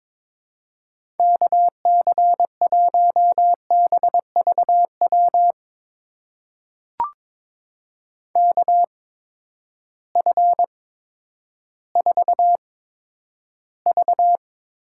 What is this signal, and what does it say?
Morse code, heard as a keyed tone, about 21 wpm, spaced as for 11 wpm, 700 Hz. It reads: KC1BVW KF4V
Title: USA Call Signs - 6 Characters - 8x Character Spacing - 22wpm